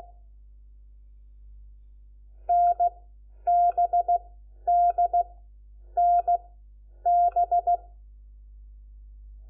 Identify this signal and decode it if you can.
Morse code, heard as a keyed tone, about 16 wpm, 700 Hz. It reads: N B D N B
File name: Lecciones / CLASE Nº4